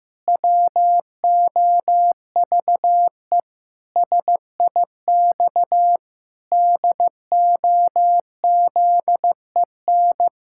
Morse code, heard as a keyed tone, about 15 wpm, 700 Hz. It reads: WOVE SIX DOZEN
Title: Pangram Challenge v2 - 15wpm